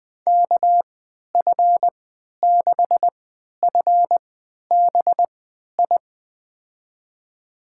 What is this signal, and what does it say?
Morse code, heard as a keyed tone, about 20 wpm, spaced as for 11 wpm, 700 Hz. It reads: KF6FBI
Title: USA Call Signs - 6 Characters - 3x Character Spacing - 20wpm